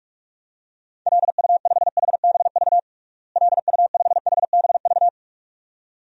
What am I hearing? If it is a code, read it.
Morse code, heard as a keyed tone, about 42 wpm, 700 Hz. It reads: LU5HBV LU5HBV
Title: DXCC-100 Call Signs - 6 Characters - Repeated 2x - 45wpm